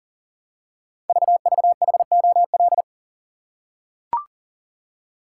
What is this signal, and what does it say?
Morse code, heard as a keyed tone, about 38 wpm, 690 Hz. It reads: VVHOL
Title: Code Groups - 5 Letters - 40wpm